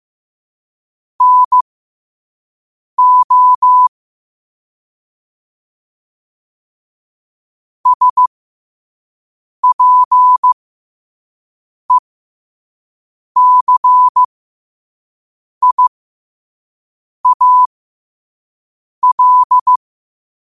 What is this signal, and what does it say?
Morse code, heard as a keyed tone, about 15 wpm, spaced as for 5 wpm, 1000 Hz. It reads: NO SPECIAL
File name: qotdfiles / 5wpm